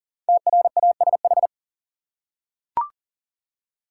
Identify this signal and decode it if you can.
Morse code, heard as a keyed tone, about 39 wpm, 700 Hz. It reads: TRASH
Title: Five Letter Words - 40wpm